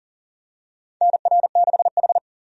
Morse code, heard as a keyed tone, about 40 wpm, 700 Hz. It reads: NR6H
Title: USA Call Signs - 4 Characters - 40wpm